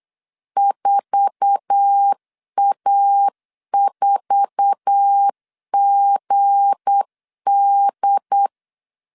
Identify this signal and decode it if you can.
Morse code, heard as a keyed tone, about 8 wpm, 800 Hz. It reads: 4A4GD
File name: LEZIONI / 8 YJ45 / 8 YJ4 8wpm 3246